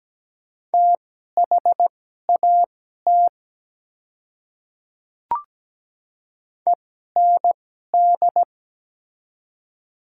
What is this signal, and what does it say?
Morse code, heard as a keyed tone, about 17 wpm, spaced as for 12 wpm, 700 Hz. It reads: THAT END
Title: 2x Character Spacing - Top 200 Words - 17wpm